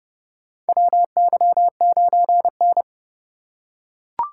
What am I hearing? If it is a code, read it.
Morse code, heard as a keyed tone, about 30 wpm, 700 Hz. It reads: WY9N
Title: USA Call Signs - 8x Character Spacing - 30wpm